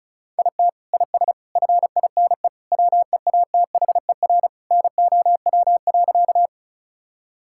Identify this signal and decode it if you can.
Morse code, heard as a keyed tone, about 36 wpm, 700 Hz. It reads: IT IS FINE WEATHER NOW.
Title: Sentences from Top 1000 Words - Review Part 5 - 35wpm